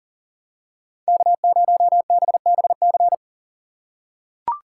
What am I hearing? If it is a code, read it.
Morse code, heard as a keyed tone, about 40 wpm, 700 Hz. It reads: K0BBC
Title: USA Call Signs - 5 Characters - 3x Character Spacing - 40wpm